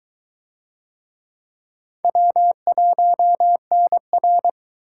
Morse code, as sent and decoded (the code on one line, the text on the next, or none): .-- .---- -. .-.
W1NR